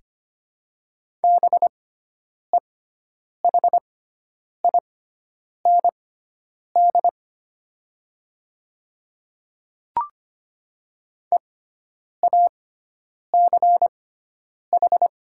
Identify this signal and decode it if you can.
Morse code, heard as a keyed tone, about 25 wpm, spaced as for 9 wpm, 700 Hz. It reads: BEHIND EACH
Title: 6x Character Spacing - Top 500 Words - 25wpm